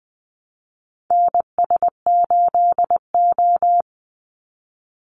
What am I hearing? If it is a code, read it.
Morse code, heard as a keyed tone, about 20 wpm, 700 Hz. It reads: NS8O
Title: Call Signs - 20wpm